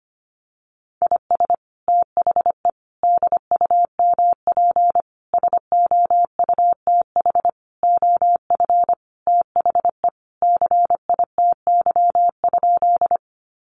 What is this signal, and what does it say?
Morse code, heard as a keyed tone, about 25 wpm, 700 Hz. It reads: IS THE DUMP SOUTH OF THE CITY?
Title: Sentences from Top 1300 Words 25wpm